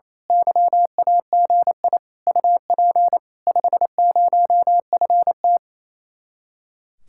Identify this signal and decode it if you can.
Morse code, heard as a keyed tone, about 27 wpm, 700 Hz. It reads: YAGI UP 50FT
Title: Speed-Racing Common QSO Phrases - 28wpm